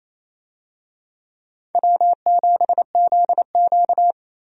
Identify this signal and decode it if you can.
Morse code, heard as a keyed tone, about 28 wpm, 700 Hz. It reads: W7ZQ
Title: USA Call Signs - 4 Characters - 28wpm